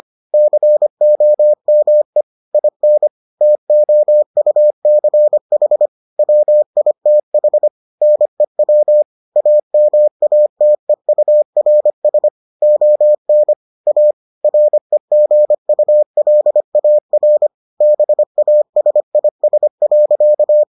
Morse code, heard as a keyed tone, about 25 wpm, 600 Hz. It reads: COME IN TOUCH WITH NEW AMATEURS ON A REGULAR BASIS.